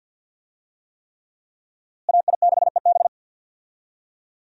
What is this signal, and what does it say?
Morse code, heard as a keyed tone, about 47 wpm, 700 Hz. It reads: AI6EB
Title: USA Call Signs - 5 Characters - Repeated 3x - 50wpm